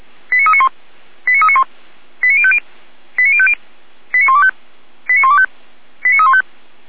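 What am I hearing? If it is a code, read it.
Five-tone selective call (ZVEI), heard as quick runs of five tones, tones of about 70 ms. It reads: 89391 89391 89050 89050 89125 89125 89315
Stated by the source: ZVEI five-tone selective call received on SDR.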